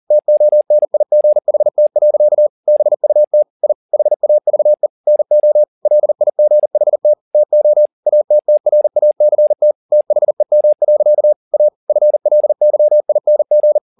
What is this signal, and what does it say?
Morse code, heard as a keyed tone, about 40 wpm, 600 Hz. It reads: TONIGHT. BUT I HAVE NO LIGHT TO ATTRACT THEM. A FLYING